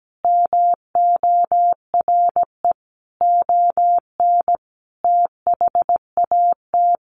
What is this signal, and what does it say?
Morse code, heard as a keyed tone, about 17 wpm, 700 Hz. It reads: MORE ON THAT
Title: Sentences from Top 800 Words 17wpm